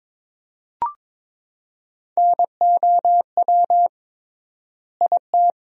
Morse code, heard as a keyed tone, about 22 wpm, 700 Hz. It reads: NOW IT
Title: 3x Word-Spacing - Sentences from Top 200 Words - 22wpm